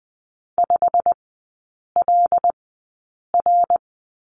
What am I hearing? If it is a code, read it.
Morse code, heard as a keyed tone, about 20 wpm, spaced as for 8 wpm, 690 Hz. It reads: 5LR
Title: 159 Lesson 26e - Letter g - Three Characters 20wpm